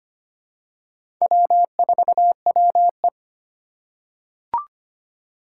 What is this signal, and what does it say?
Morse code, heard as a keyed tone, about 25 wpm, 700 Hz. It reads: W4WE
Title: Four Challenging Letter-Number Combinations 25wpm